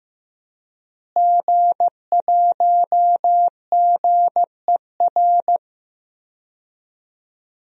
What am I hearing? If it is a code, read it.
Morse code, heard as a keyed tone, about 15 wpm, 700 Hz. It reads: G1GER
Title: DXCC-100 Call Signs - 5 Characters - Repeated 3x - 15wpm